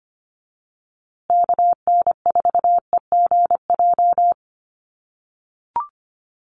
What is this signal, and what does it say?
Morse code, heard as a keyed tone, about 25 wpm, 700 Hz. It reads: KN4EGJ
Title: USA Call Signs Repeated 3x 25wpm